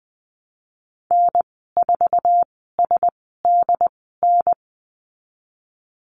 Morse code, heard as a keyed tone, about 20 wpm, spaced as for 15 wpm, 700 Hz. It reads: N4SDN
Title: 076 Lesson 13g - Letter L - Call Signs 20wpm